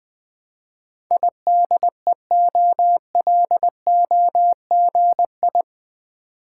Morse code, heard as a keyed tone, about 20 wpm, 700 Hz. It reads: IDEOLOGI
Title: Build A Word - 4 Letters At A Time - 11-Letter Words - 20wpm